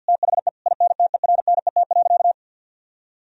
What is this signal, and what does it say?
Morse code, heard as a keyed tone, about 50 wpm, 700 Hz. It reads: THE INTERNET.
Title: Sentences from Top 1000 Words - Review Part 4 - 50wpm